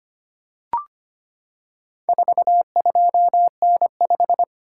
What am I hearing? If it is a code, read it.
Morse code, heard as a keyed tone, about 25 wpm, 700 Hz. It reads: 42N5
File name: practice / 25wpm / Four Letter-Number Combinations - 25wpm